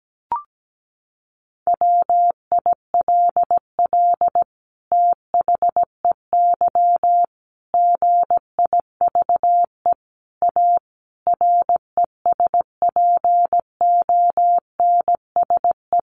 Morse code, heard as a keyed tone, about 17 wpm, 700 Hz. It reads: WILL THEY GIVE A RESPONSE
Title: Sentences from Top 1400 Words 17wpm